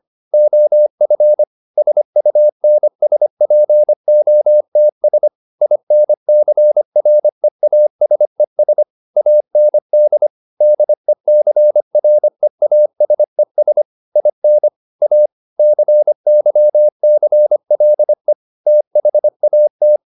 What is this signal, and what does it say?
Morse code, heard as a keyed tone, about 25 wpm, 600 Hz. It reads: OF SUNSPOTS INCREASES AND DECREASES IN A CYCLE THAT